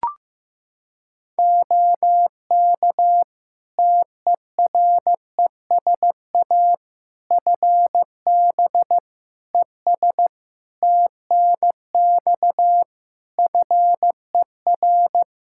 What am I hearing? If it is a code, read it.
Morse code, heard as a keyed tone, about 15 wpm, 700 Hz. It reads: OK TERESA FB ES TNX FER
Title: Practice QSO - Exchange 2 - Info - 15wpm